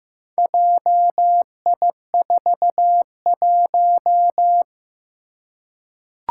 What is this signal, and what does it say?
Morse code, heard as a keyed tone, about 15 wpm, 700 Hz. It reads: JI41
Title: Four Challenging Letter-Number Combinations 15wpm